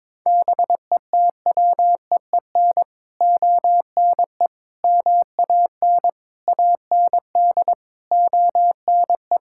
Morse code, heard as a keyed tone, about 22 wpm, 700 Hz. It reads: BETWEEN ONE MAN AND ONE
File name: practice / 22wpm / Sets of 6 Words - 22wpm